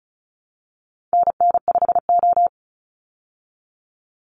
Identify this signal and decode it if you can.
Morse code, heard as a keyed tone, about 34 wpm, 700 Hz. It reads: NN5O
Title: Call Signs Repeated 3x - 35wpm